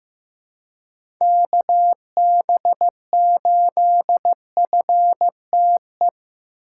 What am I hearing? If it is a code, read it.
Morse code, heard as a keyed tone, about 16 wpm, 700 Hz. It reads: KB8FTE
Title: USA Call Signs - 6 Characters - 2x Character Spacing - 15wpm